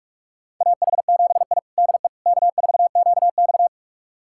Morse code, heard as a keyed tone, about 46 wpm, 700 Hz. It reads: AH7I DE K4QX
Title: USA Callsign DE Callsign - 4 Characters - 45wpm